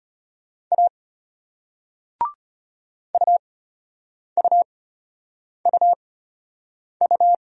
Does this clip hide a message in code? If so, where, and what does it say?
Yes: Morse code from 0.7 to 7.3 s, heard as a keyed tone, about 34 wpm, spaced as for 8 wpm, 700 Hz. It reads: A UUUU